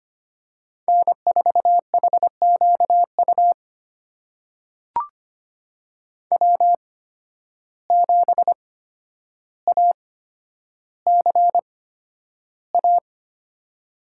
Morse code, heard as a keyed tone, about 25 wpm, spaced as for 11 wpm, 700 Hz. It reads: N4HQU W7ACA
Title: USA Call Signs - 5 Characters - 8x Character Spacing - 25wpm